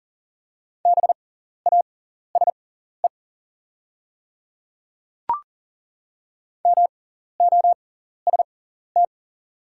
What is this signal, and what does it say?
Morse code, heard as a keyed tone, about 40 wpm, spaced as for 14 wpm, 690 Hz. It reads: BASE MOST
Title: 6x Character Spacing - Top 300 Words - 40wpm